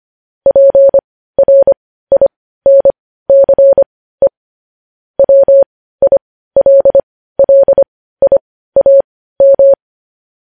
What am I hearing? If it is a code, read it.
Morse code, heard as a keyed tone, about 25 wpm, spaced as for 15 wpm, 550 Hz. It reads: PRINCE WILLIAM
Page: News Headlines in Morse Code at 15 WPM